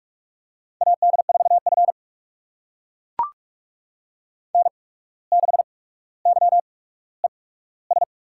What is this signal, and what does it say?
Morse code, heard as a keyed tone, about 42 wpm, spaced as for 18 wpm, 700 Hz. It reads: AD4F N6YES
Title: USA Call Signs - 8x Character Spacing - 45wpm